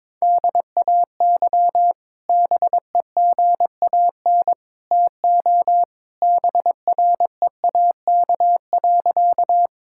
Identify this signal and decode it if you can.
Morse code, heard as a keyed tone, about 22 wpm, 700 Hz. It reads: DAY BEGAN TO BREAK.
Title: Sentences from Top 1000 Words - Review Part 6 - 22wpm